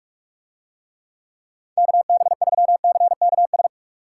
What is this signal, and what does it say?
Morse code, heard as a keyed tone, about 47 wpm, 700 Hz. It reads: KB3CKS